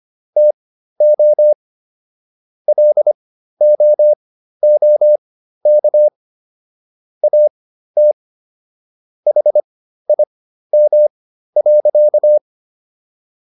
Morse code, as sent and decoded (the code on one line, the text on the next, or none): - --- / .-.. --- --- -.- / .- - / .... .. -- .-.-.-
TO LOOK AT HIM.